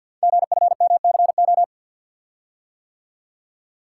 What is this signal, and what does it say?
Morse code, heard as a keyed tone, about 48 wpm, 700 Hz. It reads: GFMCO
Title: Code Groups - 5 Letters - 50wpm